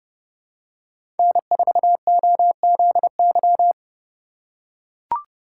Morse code, heard as a keyed tone, about 30 wpm, 700 Hz. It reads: N4OZY
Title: USA Call Signs - 5 Characters - 3x Character Spacing - 30wpm